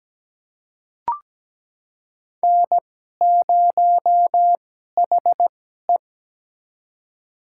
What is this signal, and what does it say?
Morse code, heard as a keyed tone, about 17 wpm, spaced as for 12 wpm, 700 Hz. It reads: N0HE
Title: USA Call Signs - 4 Characters - 2x Character Spacing - 17wpm